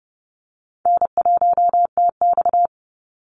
Morse code, heard as a keyed tone, about 30 wpm, 700 Hz. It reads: N1TX
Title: Easy Call Signs Repeated 3x - 30wpm